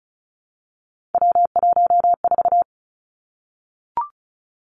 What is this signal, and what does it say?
Morse code, heard as a keyed tone, about 36 wpm, 700 Hz. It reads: W14
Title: Three Challenging Letter-Number Combinations 35wpm